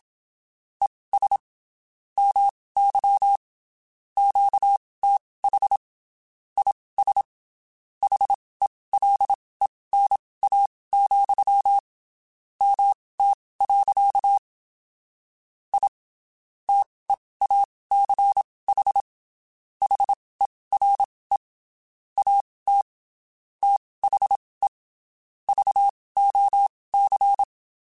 Morse code, heard as a keyed tone, about 27 wpm, spaced as for 19 wpm, 800 Hz. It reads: ES MY QTH IS HELENA, MT. I TEACH HERE AT THE VOC
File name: (spaced) / QSOs Spaced-mp3s / WD0BGP 1 @25 sp